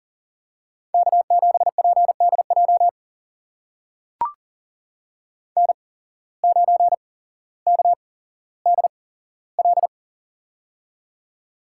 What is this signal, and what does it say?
Morse code, heard as a keyed tone, about 39 wpm, spaced as for 17 wpm, 700 Hz. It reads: K7PDJ N9KDL
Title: USA Call Signs - 5 Characters - 8x Character Spacing - 40wpm